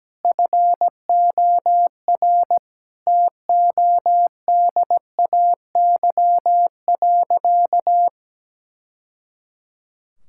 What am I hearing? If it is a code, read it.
Morse code, heard as a keyed tone, about 17 wpm, 700 Hz. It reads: FOR TODAY.
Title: Sentences from Top 1000 Words - Review Part 5 - 17wpm